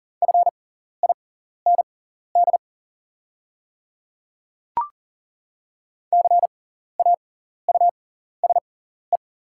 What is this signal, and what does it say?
Morse code, heard as a keyed tone, about 40 wpm, spaced as for 14 wpm, 690 Hz. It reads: FIND CAUSE
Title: 6x Character Spacing - Top 200 Words - 40wpm